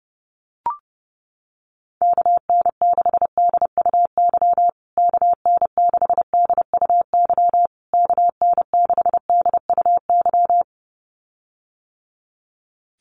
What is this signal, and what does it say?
Morse code, heard as a keyed tone, about 30 wpm, 700 Hz. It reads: KN6DUY KN6DUY KN6DUY
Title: USA Call Signs Repeated 3x 30wpm